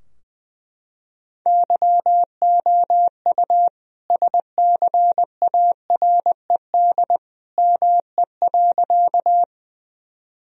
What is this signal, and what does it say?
Morse code, heard as a keyed tone, about 20 wpm, 700 Hz. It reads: YOU SCARED ME.
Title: Sentences from Top 1000 Words - Review Part 4 - 20wpm